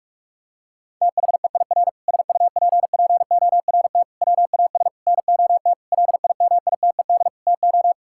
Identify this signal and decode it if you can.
Morse code, heard as a keyed tone, about 41 wpm, 700 Hz. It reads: THEIR SUPPORT WAS NOT LIMITED TO
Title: Sentences from Top 1000 Words - Review Part 6 - 45wpm